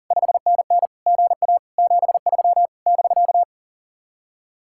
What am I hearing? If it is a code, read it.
Morse code, heard as a keyed tone, about 49 wpm, 700 Hz. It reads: 5NN GA 73 <BK>
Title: Speed-Racing Common QSO Phrases - 40wpm